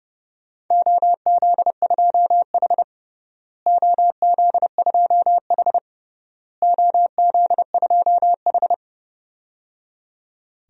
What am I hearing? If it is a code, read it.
Morse code, heard as a keyed tone, about 30 wpm, 700 Hz. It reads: OZ2H OZ2H OZ2H
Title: DXCC-100 Call Signs - 4 Characters - Repeated 3x - 30wpm